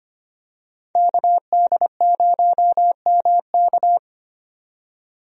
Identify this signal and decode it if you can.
Morse code, heard as a keyed tone, about 25 wpm, 700 Hz. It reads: KD0MK
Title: USA Call Signs - 5 Characters - 25wpm